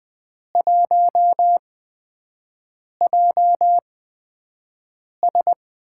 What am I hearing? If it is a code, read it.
Morse code, heard as a keyed tone, about 20 wpm, spaced as for 5 wpm, 700 Hz. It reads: 1JS
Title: USA Call Signs - 4 Characters - 8x Character Spacing - 20wpm